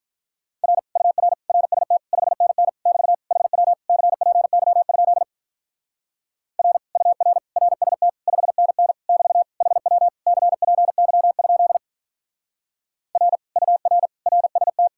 Morse code, heard as a keyed tone, about 46 wpm, 700 Hz. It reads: R UR RST 5NN <BT> HW CPY? R UR RST 5NN <BT> HW CPY? R UR RST